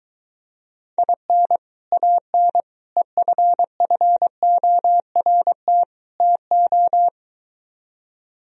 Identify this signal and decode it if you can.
Morse code, heard as a keyed tone, about 22 wpm, 700 Hz. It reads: IN AN EFFORT TO